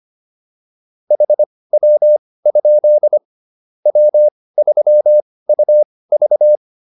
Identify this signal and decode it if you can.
Morse code, heard as a keyed tone, about 25 wpm, spaced as for 18 wpm, 600 Hz. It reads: HW? W3UV
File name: QSO403 18